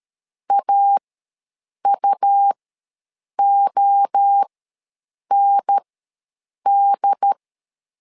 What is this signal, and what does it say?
Morse code, heard as a keyed tone, about 13 wpm, spaced as for 7 wpm, 800 Hz. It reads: AUOND
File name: LEZIONI / 3 MWRO / 3 MWRO ws 12wpm 1914